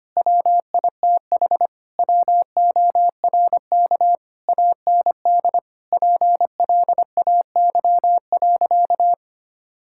Morse code, heard as a keyed tone, about 25 wpm, 700 Hz. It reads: WITH WORK AND PLAY.
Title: Sentences from Top 1000 Words - Review Part 6 - 25wpm